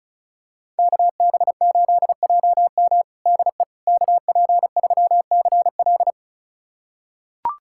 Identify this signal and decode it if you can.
Morse code, heard as a keyed tone, about 36 wpm, 700 Hz. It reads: KB8JM DE KP3CL
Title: USA Callsign DE Callsign - 5 Characters - 35wpm